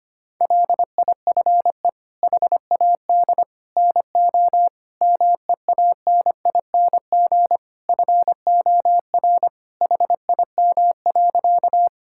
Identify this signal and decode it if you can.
Morse code, heard as a keyed tone, about 25 wpm, 700 Hz. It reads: LIFE HAD NO MEANING FOR HIM.
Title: Sentences from Top 1000 Words - Review Part 6 - 25wpm